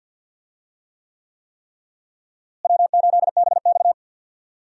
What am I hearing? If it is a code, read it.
Morse code, heard as a keyed tone, about 47 wpm, 700 Hz. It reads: W8BX
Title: USA Call Signs - 4 Characters - Repeated 3x - 50wpm